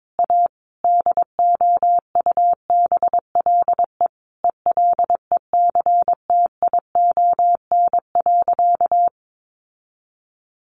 Morse code, heard as a keyed tone, about 22 wpm, 700 Hz. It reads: A DOUBLE ELECTION.
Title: Sentences from Top 1500 Words 22wpm